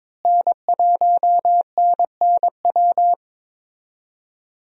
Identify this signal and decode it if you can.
Morse code, heard as a keyed tone, about 22 wpm, 700 Hz. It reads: N1NNW
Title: USA Call Signs - 5 Characters - Repeated 3x - 22wpm